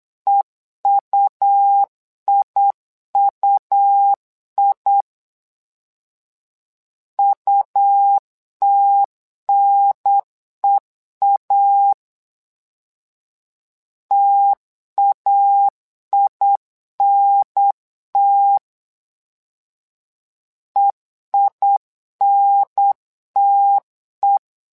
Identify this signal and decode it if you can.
Morse code, heard as a keyed tone, about 8 wpm, 800 Hz. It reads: EUIUI UTNEA TAINT EINTE
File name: LEZIONI / 2 UHDN / 2 UHDN 8wpm2232